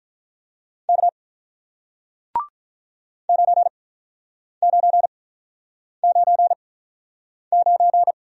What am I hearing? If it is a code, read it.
Morse code, heard as a keyed tone, about 48 wpm, spaced as for 9 wpm, 700 Hz. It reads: K 9999